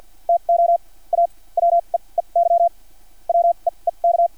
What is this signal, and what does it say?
Morse code, heard as a keyed tone, about 50 wpm, spaced as for 34 wpm, 680 Hz. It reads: TO A WEEY WEEK
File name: archived / CRSnet